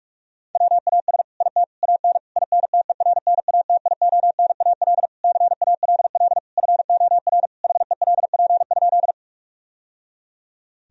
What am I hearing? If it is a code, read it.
Morse code, heard as a keyed tone, about 48 wpm, 700 Hz. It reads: WAS IT AN INTERNATIONAL CALL FOR HELP?
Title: Sentences from Top 1000 Words - Review Part 4 - 45wpm